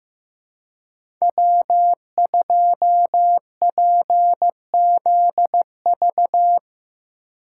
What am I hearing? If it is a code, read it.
Morse code, heard as a keyed tone, about 15 wpm, 700 Hz. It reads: W2PZV
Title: USA Call Signs - 5 Characters - 4x Character Spacing - 15wpm